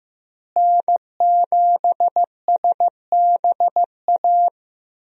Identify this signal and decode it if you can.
Morse code, heard as a keyed tone, about 16 wpm, 700 Hz. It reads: N7SBA
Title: USA Call Signs - 5 Characters - Repeated 3x - 15wpm